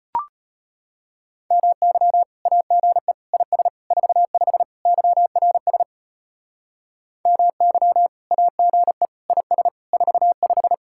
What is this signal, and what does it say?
Morse code, heard as a keyed tone, about 32 wpm, 700 Hz. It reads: MY AGE IS 45 YRS MY AGE IS 45